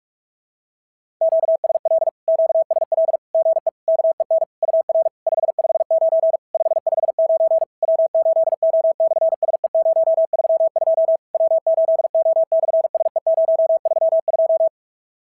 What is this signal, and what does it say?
Morse code, heard as a keyed tone, about 44 wpm, 650 Hz. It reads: QSL QSL GE KEN UR 559 559 W8O/SE032 W8O/SE032
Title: SOTA QSO Practice 45wpm